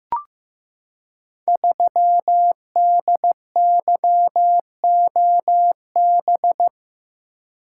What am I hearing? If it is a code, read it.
Morse code, heard as a keyed tone, about 15 wpm, 700 Hz. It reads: 3DYOB
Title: Code Groups - 5 Letter:Numbers - 15wpm